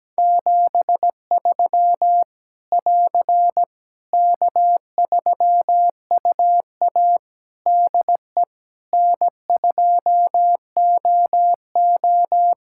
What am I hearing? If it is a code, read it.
Morse code, heard as a keyed tone, about 17 wpm, 700 Hz. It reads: 73 <AR> K3UA DE N2OO
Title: K1USN SST QSO Practice 17wpm